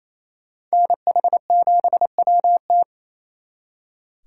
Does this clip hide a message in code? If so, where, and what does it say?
Yes: Morse code from 0.7 to 2.8 s, heard as a keyed tone, about 35 wpm, 700 Hz. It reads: NH7WT